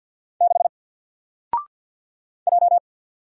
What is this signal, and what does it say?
Morse code, heard as a keyed tone, about 50 wpm, 700 Hz. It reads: 6 J
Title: Single Character (Letter:Number:Punctuation:Prosign) 50wpm